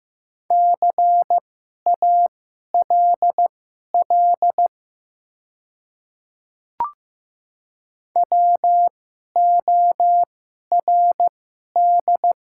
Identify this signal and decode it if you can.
Morse code, heard as a keyed tone, about 15 wpm, spaced as for 11 wpm, 700 Hz. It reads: CALL WORD
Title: 2x Character Spacing - Top 100 Words - 15wpm